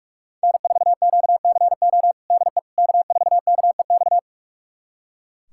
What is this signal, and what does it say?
Morse code, heard as a keyed tone, about 41 wpm, 690 Hz. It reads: N4QCO DE K4KEX
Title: USA Callsign DE Callsign 45wpm